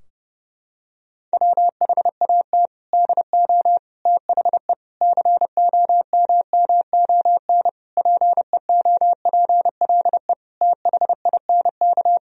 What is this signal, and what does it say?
Morse code, heard as a keyed tone, about 32 wpm, 700 Hz. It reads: WHAT DO THE COMMON PEOPLE THINK
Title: Sentences from Top 1000 Words - Review Part 3 - 30wpm